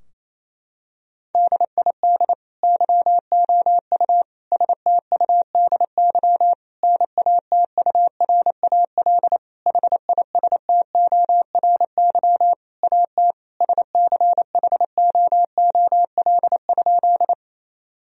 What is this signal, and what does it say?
Morse code, heard as a keyed tone, about 28 wpm, 700 Hz. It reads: DID YOU STUDY NATURAL HISTORY AT SCHOOL?
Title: Sentences from Top 1000 Words - Review Part 4 - 28wpm